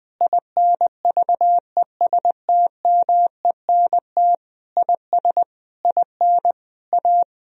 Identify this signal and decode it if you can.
Morse code, heard as a keyed tone, about 20 wpm, 700 Hz. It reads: INVESTMENT IS IN A
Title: Sentences from Top 1000 Words - Review Part 4 - 20wpm